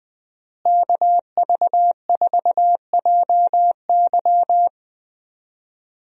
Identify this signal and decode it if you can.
Morse code, heard as a keyed tone, about 21 wpm, 700 Hz. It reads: KV4JY
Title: USA Call Signs - 5 Characters - Repeated 3x - 20wpm